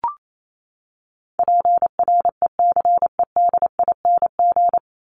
Morse code, heard as a keyed tone, about 28 wpm, 700 Hz. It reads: PRECEDING
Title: ING Suffix - Top 100 Words with Base Words 28wpm